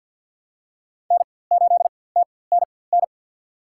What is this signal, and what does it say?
Morse code, heard as a keyed tone, about 46 wpm, spaced as for 23 wpm, 700 Hz. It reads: N8TNN
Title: USA Call Signs - 5 Characters - 4x Character Spacing - 50wpm